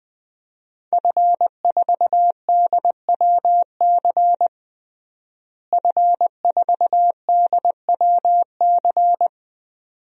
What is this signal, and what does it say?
Morse code, heard as a keyed tone, about 20 wpm, 700 Hz. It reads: F4DWC F4DWC
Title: DXCC-100 Call Signs - 5 Characters - Repeated 2x - 20wpm